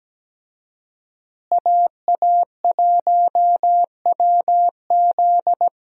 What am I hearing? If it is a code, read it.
Morse code, heard as a keyed tone, about 17 wpm, 700 Hz. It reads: AA1WZ
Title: USA Call Signs - 5 Characters - Repeated 2x - 17wpm